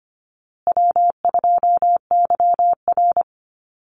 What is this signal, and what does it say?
Morse code, heard as a keyed tone, about 25 wpm, 700 Hz. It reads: W2YR
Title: Easy Call Signs Repeated 3x - 25wpm